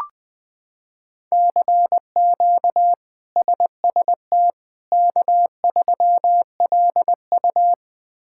Morse code, heard as a keyed tone, about 20 wpm, 700 Hz. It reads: CQ SST K3LU
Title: K1USN SST QSO Practice - 20wpm